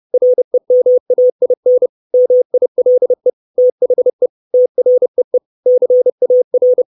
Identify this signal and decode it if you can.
Morse code, heard as a keyed tone, about 30 wpm, 500 Hz. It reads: REMAIN MILE THE TREE CAR
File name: 632 Words 30WPM